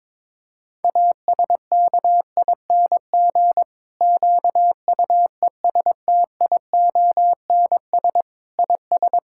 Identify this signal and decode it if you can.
Morse code, heard as a keyed tone, about 22 wpm, 700 Hz. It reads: ASKING QUESTIONS IS
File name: Difficult Sentences with Hints - 22wpm